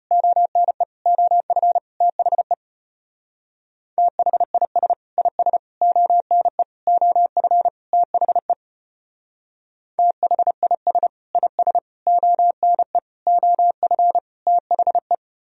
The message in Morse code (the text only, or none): ONE OF THE THIS IS ONE OF THE THIS IS ONE OF THE